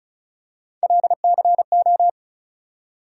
Morse code, heard as a keyed tone, about 34 wpm, 700 Hz. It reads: LCO
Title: USA Call Sign Postfixes - 3 Letters 35wpm